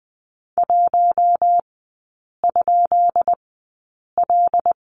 Morse code, heard as a keyed tone, about 20 wpm, spaced as for 8 wpm, 690 Hz. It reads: 1?L
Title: 159 Lesson 26e - Letter g - Three Characters 20wpm